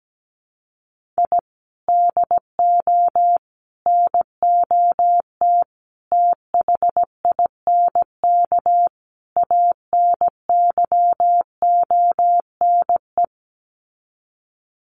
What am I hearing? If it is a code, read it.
Morse code, heard as a keyed tone, about 17 wpm, 700 Hz. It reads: I DO NOT THINK ANYONE
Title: Sets of 5 Words Encore 17wpm